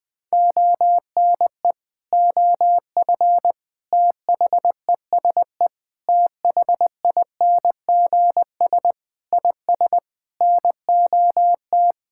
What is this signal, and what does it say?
Morse code, heard as a keyed tone, about 20 wpm, 700 Hz. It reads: ONE OF THESE THINGS IS NOT
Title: Sentences with 8-Word Phrases - 20wpm